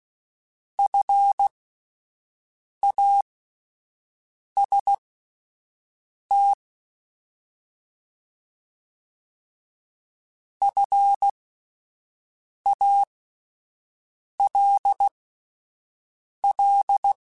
Morse code, heard as a keyed tone, about 16 wpm, spaced as for 5 wpm, 800 Hz. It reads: FAST FALL
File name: Alphabetical / Alphabetical mp3s / Fs @5